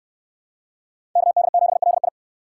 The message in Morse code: KD7LI

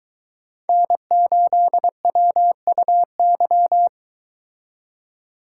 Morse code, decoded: N8WUY

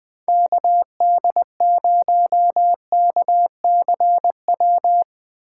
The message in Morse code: KD0KCW

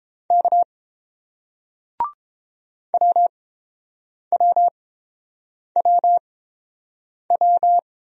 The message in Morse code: K WWWW

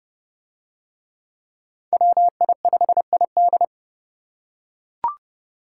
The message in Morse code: WI5ID

